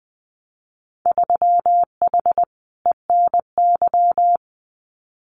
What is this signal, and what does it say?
Morse code, heard as a keyed tone, about 20 wpm, 700 Hz. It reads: 3H ENY
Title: Winter Field Day Exchanges Encore 20wpm